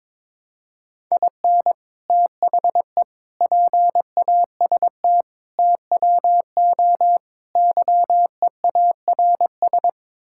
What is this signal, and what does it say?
Morse code, heard as a keyed tone, about 22 wpm, 700 Hz. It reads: IN THE PAST TWO YEARS
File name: practice / 22wpm / Build A Phrase - 5-Word Phrases - 22wpm